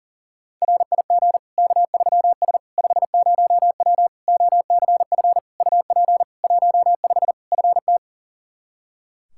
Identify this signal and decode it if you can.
Morse code, heard as a keyed tone, about 40 wpm, 700 Hz. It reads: RIG K3S 50W OCF UP 15 FT
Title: K1USN SST QSO Practice 40wpm